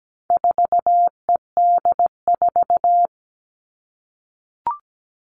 Morse code, heard as a keyed tone, about 17 wpm, 700 Hz. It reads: 4ED4
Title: All Cut Numbers - 4-Digit 17wpm